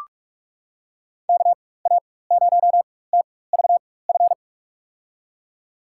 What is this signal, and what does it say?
Morse code, heard as a keyed tone, about 46 wpm, spaced as for 21 wpm, 700 Hz. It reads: KA0TVF
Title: USA Call Signs - 6 Characters - 4x Character Spacing - 45wpm